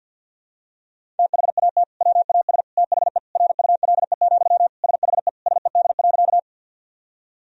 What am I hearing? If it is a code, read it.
Morse code, heard as a keyed tone, about 51 wpm, 700 Hz. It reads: THAT WAS THE RULE, SHE SED.
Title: Sentences from Top 400 Words and Abbreviations 50wpm